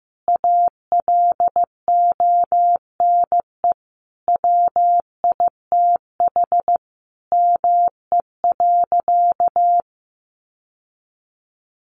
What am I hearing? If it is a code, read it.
Morse code, heard as a keyed tone, about 15 wpm, 700 Hz. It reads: ALONE WITH ME.
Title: Sentences from Top 1400 Words 15wpm